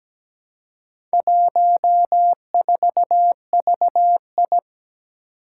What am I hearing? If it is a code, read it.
Morse code, heard as a keyed tone, about 17 wpm, 700 Hz. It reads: 14VI
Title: Four Challenging Letter-Number Combinations 17wpm